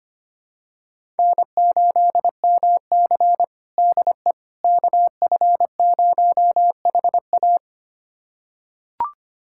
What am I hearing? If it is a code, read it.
Morse code, heard as a keyed tone, about 25 wpm, 700 Hz. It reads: N8MC DE KF0HA
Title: USA Callsign DE Callsign 25wpm